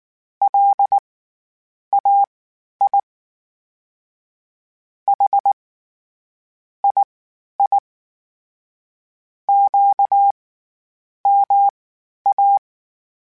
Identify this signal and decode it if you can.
Morse code, heard as a keyed tone, about 19 wpm, spaced as for 8 wpm, 800 Hz. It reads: LAI HII QMA
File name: CW Comparisons / CW Comparisons-mp3s / CW Comparisons Broken Characters @18